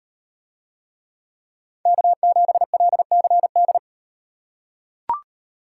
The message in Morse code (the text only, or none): K7LCD